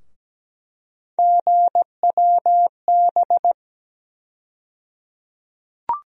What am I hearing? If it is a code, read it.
Morse code, heard as a keyed tone, about 17 wpm, 700 Hz. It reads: GWB
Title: USA Call Sign Postfixes - 3 Letters 17wpm